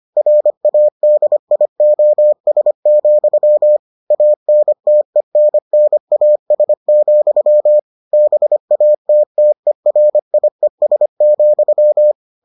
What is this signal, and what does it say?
Morse code, heard as a keyed tone, about 25 wpm, 600 Hz. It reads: RADIOS, ANTENNAS, BATTERIES,